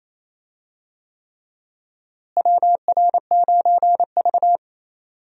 Morse code, heard as a keyed tone, about 28 wpm, 700 Hz. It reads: WR9V
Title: USA Call Signs - 4 Characters - Repeated 3x - 28wpm